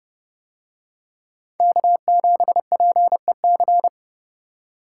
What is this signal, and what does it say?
Morse code, heard as a keyed tone, about 29 wpm, 700 Hz. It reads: K7PEC